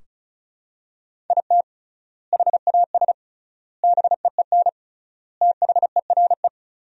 Morse code, heard as a keyed tone, about 34 wpm, 700 Hz. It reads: IT HAS BEEN THERE
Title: 3x Word-Spacing - Sentences from Top 400 Words - 35wpm